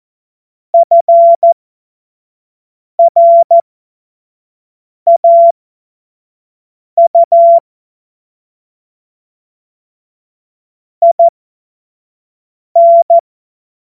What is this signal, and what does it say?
Morse code, heard as a keyed tone, about 14 wpm, spaced as for 5 wpm, 680 Hz. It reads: FRAU IN
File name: Morsekurs / Witze / Witz 15. Februar 2022 25 F70